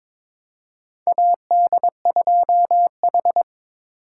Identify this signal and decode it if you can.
Morse code, heard as a keyed tone, about 22 wpm, 700 Hz. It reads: AD2H
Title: USA Call Signs - 4 Characters - 8x Character Spacing - 22wpm